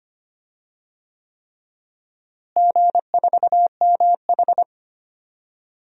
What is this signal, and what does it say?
Morse code, heard as a keyed tone, about 25 wpm, 700 Hz. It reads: G4MH
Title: DXCC-100 Call Signs - 4 Characters - Repeated 3x - 25wpm